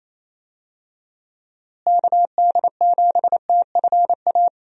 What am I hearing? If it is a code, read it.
Morse code, heard as a keyed tone, about 28 wpm, 700 Hz. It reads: KD7TFA
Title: USA Call Signs - 6 Characters - Repeated 3x - 28wpm